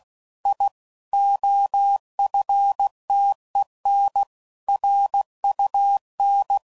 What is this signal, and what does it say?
Morse code, heard as a keyed tone, about 16 wpm, 800 Hz. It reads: I OFTEN RUN
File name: CW Code Groups @15WPM / QSOs / kc7sac